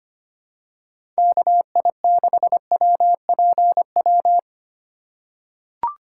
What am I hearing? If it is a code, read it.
Morse code, heard as a keyed tone, about 25 wpm, 700 Hz. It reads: KI6WPW
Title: USA Call Signs - 6 Characters - 3x Character Spacing - 25wpm